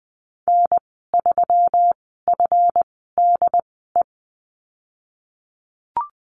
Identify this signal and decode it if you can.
Morse code, heard as a keyed tone, about 20 wpm, spaced as for 14 wpm, 700 Hz. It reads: N3FDE
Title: 173 Lesson 28g - Number 9 - Call Signs 20wpm